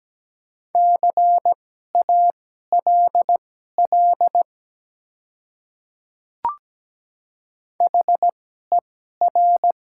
Morse code, heard as a keyed tone, about 17 wpm, spaced as for 12 wpm, 700 Hz. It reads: CALL HER
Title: 2x Character Spacing - Top 300 Words - 17wpm